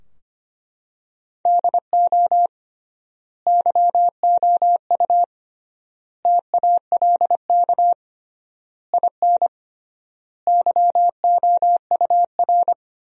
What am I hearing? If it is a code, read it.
Morse code, heard as a keyed tone, about 25 wpm, 700 Hz. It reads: DO YOU TALK IN YOUR
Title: 3x Word-Spacing - Sentences from Top 500 Words - 25wpm